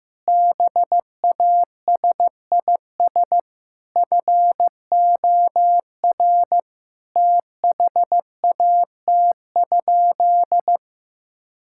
Morse code, heard as a keyed tone, about 16 wpm, 700 Hz. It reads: BASIS FOR THAT?
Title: Sentences from Top 1000 Words - Review Part 5 - 15wpm